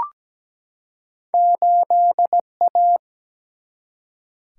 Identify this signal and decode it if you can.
Morse code, heard as a keyed tone, about 17 wpm, 700 Hz. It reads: 8A
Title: Single-Letter Single-Number 17wpm